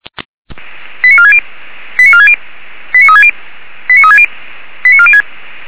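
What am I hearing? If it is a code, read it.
Five-tone selective call (ZVEI), heard as quick runs of five tones, tones of about 70 ms. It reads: 89460 89460 89360 89360 89406
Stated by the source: ZVEI five-tone selective call received by SDR.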